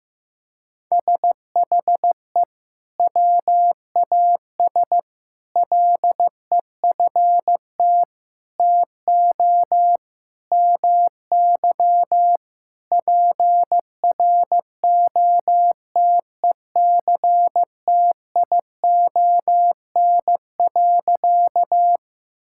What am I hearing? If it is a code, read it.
Morse code, heard as a keyed tone, about 15 wpm, 700 Hz. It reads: SHE WAS LEFT TO MY PROTECTION.
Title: Sentences from Top 1600 Words 15wpm